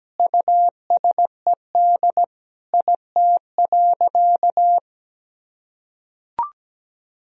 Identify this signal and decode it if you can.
Morse code, heard as a keyed tone, about 17 wpm, 700 Hz. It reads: USED IT.
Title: Sentences from Top 1000 Words - Review Part 2 - 17wpm